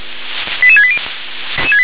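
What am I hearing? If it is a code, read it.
Five-tone selective call (ZVEI), heard as quick runs of five tones, tones of about 70 ms. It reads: 88680
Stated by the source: ZVEI five-tone selective call received on an SDR.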